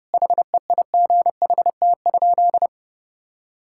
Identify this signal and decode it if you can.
Morse code, heard as a keyed tone, about 30 wpm, 700 Hz. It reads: HEIGHT?
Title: Sentences from Top 1600 Words 30wpm